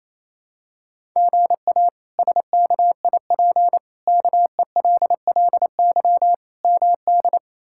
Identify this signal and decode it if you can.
Morse code, heard as a keyed tone, about 28 wpm, 700 Hz. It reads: GA SKIP KELLY MD
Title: K1USN SST QSO Practice - 28wpm